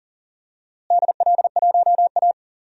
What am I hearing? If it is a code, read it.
Morse code, heard as a keyed tone, about 40 wpm, 700 Hz. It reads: DL1A
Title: DXCC-100 Call Signs - 4 Characters - Repeated 3x - 40wpm